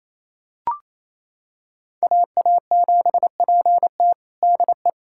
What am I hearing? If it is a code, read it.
Morse code, heard as a keyed tone, about 27 wpm, 700 Hz. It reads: AA7PT DE
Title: USA Callsign DE Callsign - 5 Characters - 28wpm